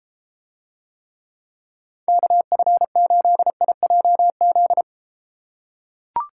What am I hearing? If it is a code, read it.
Morse code, heard as a keyed tone, about 32 wpm, 700 Hz. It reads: KF8IJZ